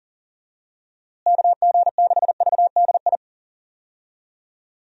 Morse code, heard as a keyed tone, about 37 wpm, 700 Hz. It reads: KG6VDI
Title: USA Call Signs - 6 Characters - Repeated 3x - 40wpm